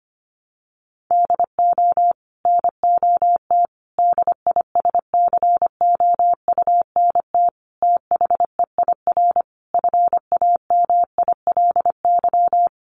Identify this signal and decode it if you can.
Morse code, heard as a keyed tone, about 25 wpm, 700 Hz. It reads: DO NOT DISCOUNT THEIR FAMILY
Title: Sentences from Top 1200 Words 25wpm